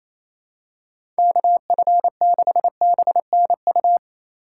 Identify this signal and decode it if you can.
Morse code, heard as a keyed tone, about 28 wpm, 700 Hz. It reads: KF6BNU
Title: USA Call Signs - 6 Characters - 4x Character Spacing - 28wpm